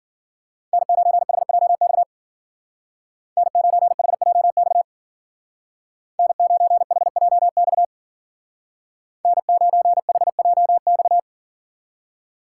Morse code, heard as a keyed tone, about 53 wpm, 700 Hz. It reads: N9HJX N9HJX N9HJX N9HJX